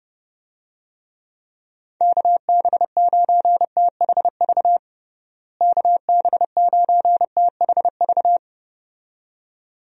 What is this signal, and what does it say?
Morse code, heard as a keyed tone, about 30 wpm, 700 Hz. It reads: KB9THV KB9THV
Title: USA Call Signs - 6 Characters - Repeated 2x - 30wpm